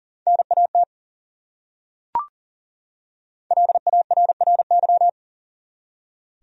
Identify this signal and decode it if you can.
Morse code, heard as a keyed tone, about 40 wpm, 700 Hz. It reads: NAT LARRY
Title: USA Common Names - Max 5 Letters - 40wpm